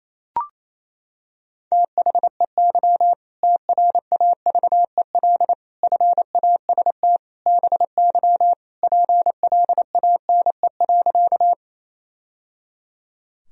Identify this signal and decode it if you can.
Morse code, heard as a keyed tone, about 28 wpm, 700 Hz. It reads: THEY TRAVEL FAST BY PLANE.
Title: Sentences from Top 1000 Words - Review Part 3 - 28wpm